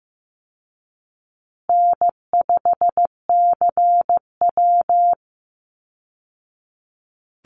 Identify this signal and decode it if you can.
Morse code, heard as a keyed tone, about 15 wpm, 700 Hz. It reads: N5CW
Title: Call Signs Repeated 3x - 15wpm